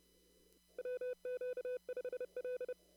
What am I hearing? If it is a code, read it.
Morse code, heard as a keyed tone, about 30 wpm, 500 Hz. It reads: WQ5L
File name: CW / kt5j wpx cw-op-k5pi